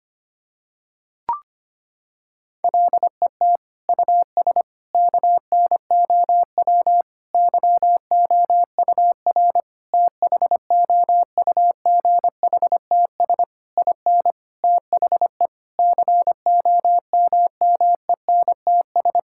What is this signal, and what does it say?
Morse code, heard as a keyed tone, about 25 wpm, 700 Hz. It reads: LET US KNOW YOUR THOUGHTS IN THE COMMENTS
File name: Sets of 9 Words - 25wpm